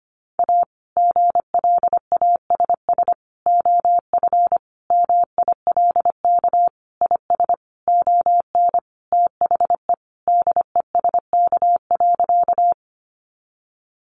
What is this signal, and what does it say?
Morse code, heard as a keyed tone, about 25 wpm, 700 Hz. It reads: A GLASS OF MILK IS ON THE DESK.
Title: Additional Easy Phrases 25wpm